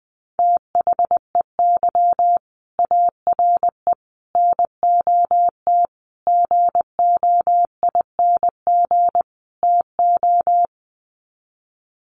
Morse code, heard as a keyed tone, about 19 wpm, 700 Hz. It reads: THEY ARE NOT GOING TO
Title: Sets of 5 Words Encore2 20wpm